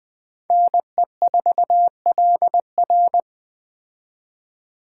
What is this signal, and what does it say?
Morse code, heard as a keyed tone, about 20 wpm, 700 Hz. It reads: NE4LR
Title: Speed Racing - USA 2x2 Call Signs 17wpm